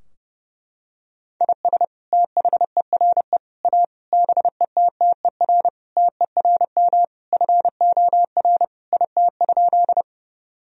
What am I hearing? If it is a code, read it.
Morse code, heard as a keyed tone, about 32 wpm, 700 Hz. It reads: IS THERE A BETTER TERM FOR IT?
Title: Sentences from Top 1000 Words - Review Part 4 - 30wpm